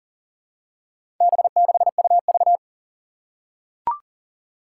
Morse code, heard as a keyed tone, about 40 wpm, 700 Hz. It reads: B6UV
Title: Four Challenging Letter-Number Combinations 40wpm